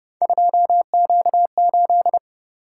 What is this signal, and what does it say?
Morse code, heard as a keyed tone, about 31 wpm, 700 Hz. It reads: 2Q8
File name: Speed-Racing - Three Letter-Number Combinations - 25wpm